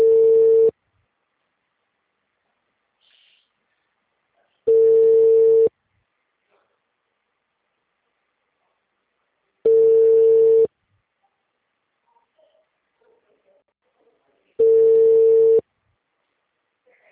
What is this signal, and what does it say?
Ringback tone (China, 450 Hz).